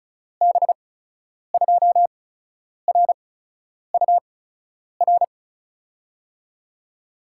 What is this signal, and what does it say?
Morse code, heard as a keyed tone, about 36 wpm, spaced as for 10 wpm, 700 Hz. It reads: B2RUR
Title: USA Call Signs - 8x Character Spacing - 35wpm